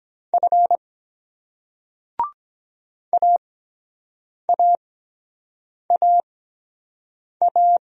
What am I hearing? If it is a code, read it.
Morse code, heard as a keyed tone, about 23 wpm, spaced as for 7 wpm, 700 Hz. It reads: F AAAA